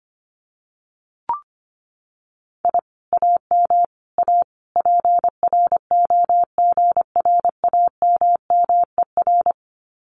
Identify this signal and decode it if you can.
Morse code, heard as a keyed tone, about 25 wpm, 700 Hz. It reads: I AM A PROGRAMMER
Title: Common QSO Phrases Encore 25wpm